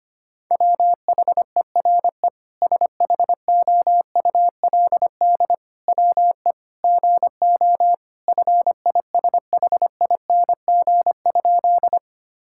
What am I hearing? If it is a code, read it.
Morse code, heard as a keyed tone, about 25 wpm, 700 Hz. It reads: WHERE SHOULD WE GO FISHING?
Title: Sentences from Top 1000 Words - Review Part 5 - 25wpm